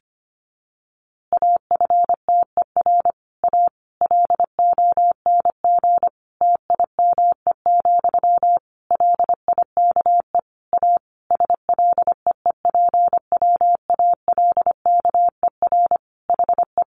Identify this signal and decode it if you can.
Morse code, heard as a keyed tone, about 25 wpm, 700 Hz. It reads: AFTER A LONG TIME, LIKE A SLEEPWALKER HE
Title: A Sound Of Thunder 25wpm